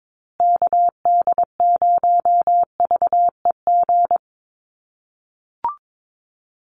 Morse code, heard as a keyed tone, about 22 wpm, 700 Hz. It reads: KD0VEG
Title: USA Call Signs Repeated 2x 22wpm